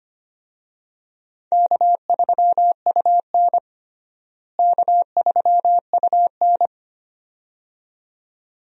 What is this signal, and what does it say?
Morse code, heard as a keyed tone, about 25 wpm, 700 Hz. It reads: K3UN K3UN
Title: USA Call Signs - 4 Characters - Repeated 2x - 25wpm